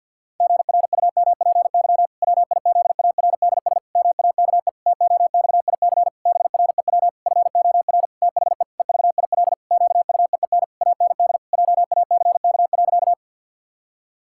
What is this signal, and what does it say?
Morse code, heard as a keyed tone, about 52 wpm, 700 Hz. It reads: GRUMPY WIZARDS MAKE TOXIC BREW FOR THE EVIL QUEEN AND JACK.
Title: Pangram Challenge v2 - 50wpm